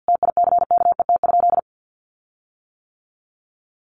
Morse code, heard as a keyed tone, about 48 wpm, 700 Hz. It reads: TICKET?
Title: Sentences from Top 1400 Words 50wpm